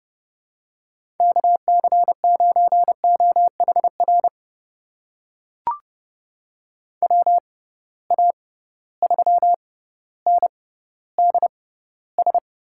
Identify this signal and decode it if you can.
Morse code, heard as a keyed tone, about 30 wpm, spaced as for 15 wpm, 700 Hz. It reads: KC9OHR WA3NDS
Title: USA Call Signs - 6 Characters - 6x Character Spacing - 30wpm